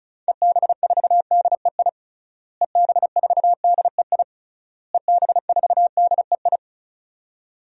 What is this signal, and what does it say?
Morse code, heard as a keyed tone, about 34 wpm, 700 Hz. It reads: EB4DEI EB4DEI EB4DEI
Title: DXCC-100 Call Signs - 6 Characters - Repeated 3x - 35wpm